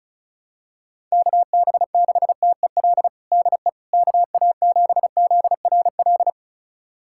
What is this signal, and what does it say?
Morse code, heard as a keyed tone, about 36 wpm, 700 Hz. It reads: KB6TEL DE KA7ZRL
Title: USA Callsign DE Callsign 35wpm